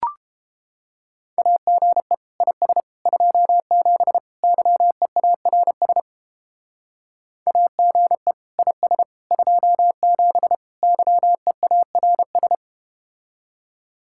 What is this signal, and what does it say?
Morse code, heard as a keyed tone, about 31 wpm, 700 Hz. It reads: AGE IS 27 YEARS AGE IS 27 YEARS